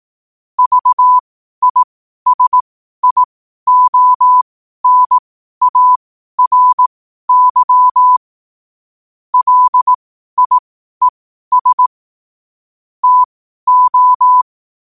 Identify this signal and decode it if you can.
Morse code, heard as a keyed tone, about 18 wpm, spaced as for 12 wpm, 1000 Hz. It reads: VISIONARY LIES TO
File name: qotdfiles / 12wpm